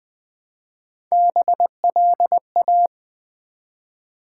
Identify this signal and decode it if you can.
Morse code, heard as a keyed tone, about 20 wpm, 700 Hz. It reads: BLA
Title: Build A Word - 1 Letter At A Time - 3 to 5-Letter Words - 20wpm